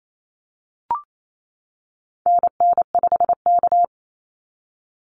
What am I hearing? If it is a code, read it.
Morse code, heard as a keyed tone, about 28 wpm, 700 Hz. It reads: NN5K
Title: NA SOTA Activator Callsigns 28wpm